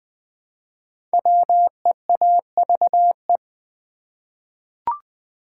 Morse code, heard as a keyed tone, about 20 wpm, 700 Hz. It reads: WEAVE
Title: 5 Letter Words - First 3 Letters Given - 20wpm